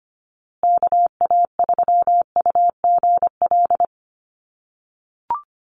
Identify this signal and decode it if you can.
Morse code, heard as a keyed tone, about 25 wpm, 700 Hz. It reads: KA3UGL
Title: USA Call Signs Repeated 2x 25wpm